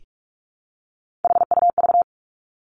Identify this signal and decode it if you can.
Morse code, heard as a keyed tone, about 46 wpm, 700 Hz. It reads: HUV
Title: Three Challenging Letter-Number Combinations 45wpm